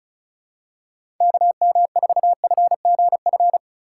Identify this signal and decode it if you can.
Morse code, heard as a keyed tone, about 34 wpm, 690 Hz. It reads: KM4FGF
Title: USA Call Signs - 6 Characters - 2x Character Spacing - 35wpm